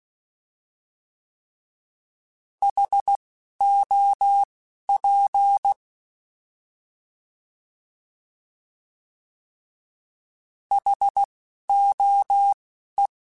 Morse code, heard as a keyed tone, about 16 wpm, spaced as for 11 wpm, 800 Hz. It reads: HOP HOE
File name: Alphabetical / Alphabetical mp3s / Hs @10